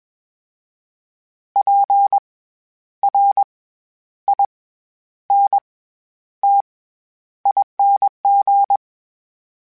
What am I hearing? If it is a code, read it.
Morse code, heard as a keyed tone, about 21 wpm, spaced as for 10 wpm, 800 Hz. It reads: PRINTING
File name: Special INGs-mp3s / Special INGs-mp3s / SlowToFaster ing @20